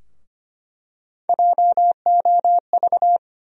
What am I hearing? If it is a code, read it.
Morse code, heard as a keyed tone, about 25 wpm, 700 Hz. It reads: JOV
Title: USA Call Sign Postfixes - 3 Letters 25wpm